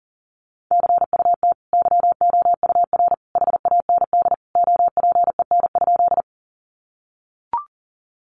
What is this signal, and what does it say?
Morse code, heard as a keyed tone, about 39 wpm, 700 Hz. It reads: CUT YOUR HAND OPEN?
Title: Sentences from Top 300 Words 40wpm